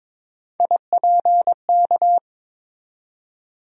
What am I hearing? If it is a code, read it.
Morse code, heard as a keyed tone, about 22 wpm, 700 Hz. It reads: IPK
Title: USA Call Sign Postfixes - 3 Letters 22wpm